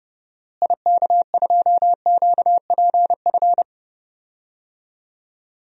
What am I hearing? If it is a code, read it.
Morse code, heard as a keyed tone, about 32 wpm, 700 Hz. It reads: IK2QPF
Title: DXCC-100 Call Signs - 6 Characters - Repeated 2x - 30wpm